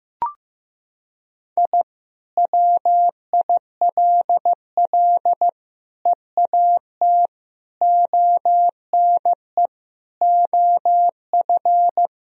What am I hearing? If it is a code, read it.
Morse code, heard as a keyed tone, about 15 wpm, 700 Hz. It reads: I WILL EAT ONE OF
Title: Sentences from Top 1000 Words - Review Part 1 - 15wpm